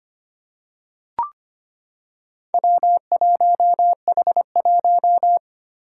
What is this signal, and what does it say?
Morse code, heard as a keyed tone, about 25 wpm, 700 Hz. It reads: W1H1
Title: Four Challenging Letter-Number Combinations 25wpm